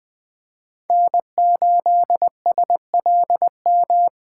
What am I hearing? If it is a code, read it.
Morse code, heard as a keyed tone, about 20 wpm, 700 Hz. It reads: N8SLM
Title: USA Call Signs - 5 Characters - 3x Character Spacing - 20wpm